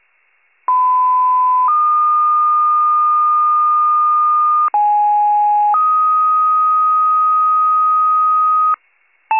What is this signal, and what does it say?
Two-tone page, heard as a one-second tone, then a three-second tone, about 1010 then 1220 Hz, about 830 then 1250 Hz. It is heard from 0.7 to 8.7 s.
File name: pager / 74 FIRE / wcalert